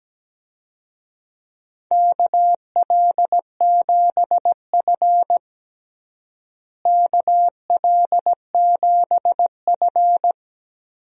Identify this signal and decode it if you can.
Morse code, heard as a keyed tone, about 17 wpm, 700 Hz. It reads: KL7F KL7F
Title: DXCC-100 Call Signs - 4 Characters - Repeated 3x - 17wpm